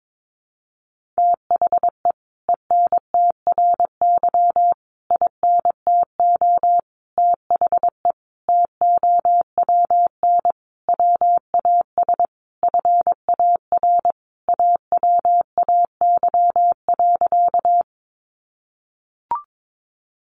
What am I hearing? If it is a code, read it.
Morse code, heard as a keyed tone, about 22 wpm, 700 Hz. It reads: THE ENTRY INTO THE TOWN WAS FAR AWAY.
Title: Sentences from Top 1500 Words 22wpm